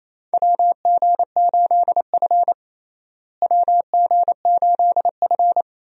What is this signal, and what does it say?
Morse code, heard as a keyed tone, about 28 wpm, 700 Hz. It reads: WG8F WG8F
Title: USA Call Signs - 4 Characters - Repeated 2x - 28wpm